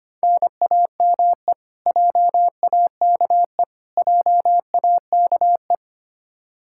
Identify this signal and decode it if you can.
Morse code, heard as a keyed tone, about 25 wpm, 700 Hz. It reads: NAME JAKE JAKE
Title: Practice QSO - Exchange 1 - Beginning - 25wpm